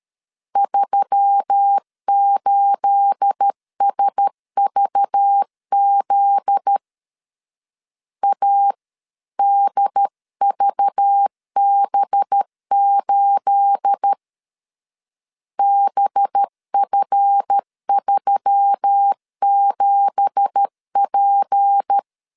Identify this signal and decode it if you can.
Morse code, heard as a keyed tone, about 13 wpm, 800 Hz. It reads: 38SVZ ADVB8 BF37P
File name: CW / DET / 1 9 1500C S B 12wpm 3819